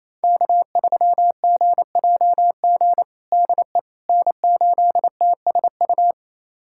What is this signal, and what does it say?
Morse code, heard as a keyed tone, about 28 wpm, 700 Hz. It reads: K3GJG DE N8TSU
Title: USA Callsign DE Callsign - 5 Characters - 28wpm